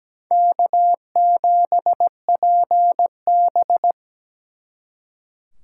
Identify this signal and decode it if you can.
Morse code, heard as a keyed tone, about 20 wpm, 700 Hz. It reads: K7PB
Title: Speed Racing - USA 1x2 Call Signs 17wpm